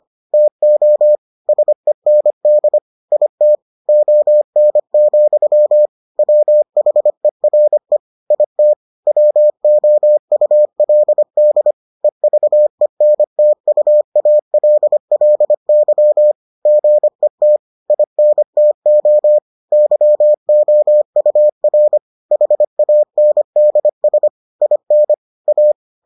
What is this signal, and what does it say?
Morse code, heard as a keyed tone, about 25 wpm, 600 Hz. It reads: TO SEND IT ON, WHERE IT WOULD EVENTUALLY GET INTO YOUR HANDS IN A